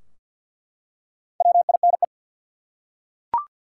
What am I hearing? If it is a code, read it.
Morse code, heard as a keyed tone, about 47 wpm, 700 Hz. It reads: WINE
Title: Four Letter Words - 50wpm